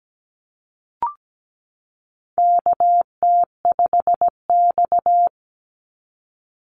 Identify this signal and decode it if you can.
Morse code, heard as a keyed tone, about 17 wpm, 700 Hz. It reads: KT5X
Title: NA SOTA Activator Callsigns 17wpm